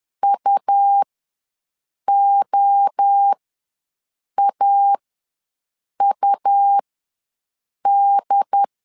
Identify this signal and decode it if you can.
Morse code, heard as a keyed tone, about 10 wpm, spaced as for 6 wpm, 800 Hz. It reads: UOAUD